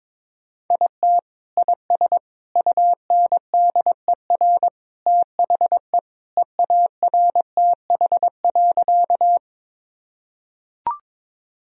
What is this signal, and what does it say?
Morse code, heard as a keyed tone, about 22 wpm, 700 Hz. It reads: IT IS UNDER THE EARTH.
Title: Sentences from Top 1000 Words - Review Part 1 - 22wpm